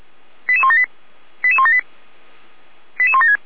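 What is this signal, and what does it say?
Five-tone selective call (ZVEI), heard as quick runs of five tones, tones of about 70 ms. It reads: 88178 88178 88176
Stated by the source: ZVEI five-tone selective call 88090 received by SDR.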